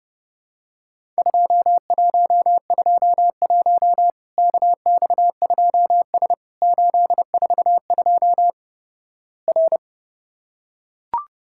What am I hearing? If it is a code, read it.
Morse code, heard as a keyed tone, about 32 wpm, 700 Hz. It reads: 2121 KX2S 842 R
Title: Worked All Europe DX Contest - QTC Exchange - 30wpm